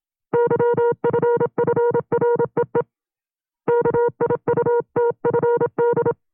Morse code, heard as a keyed tone, about 27 wpm, 470 Hz. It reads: YFFREE KIUTFD
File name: CW-practice/25WPM